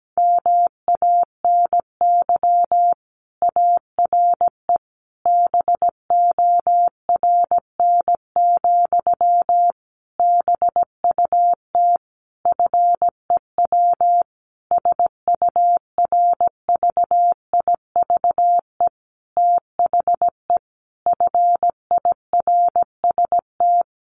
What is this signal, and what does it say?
Morse code, heard as a keyed tone, about 17 wpm, 700 Hz. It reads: MANY ARE BORN, BUT FEW SURVIVE THE FIRST
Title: Sentences from Top 1300 Words 17wpm